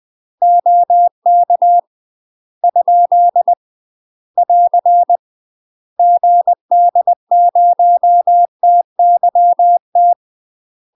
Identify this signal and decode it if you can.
Morse code, heard as a keyed tone, about 20 wpm, 700 Hz. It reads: OK ? <AR> GD0TYT
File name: UK MASTER QSOs TRAINING / 20WPM / UK MASTER QSOs 0048